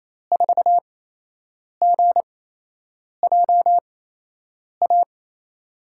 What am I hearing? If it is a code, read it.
Morse code, heard as a keyed tone, about 28 wpm, spaced as for 8 wpm, 700 Hz. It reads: 4GJA
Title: USA Call Signs - 5 Characters - 8x Character Spacing - 28wpm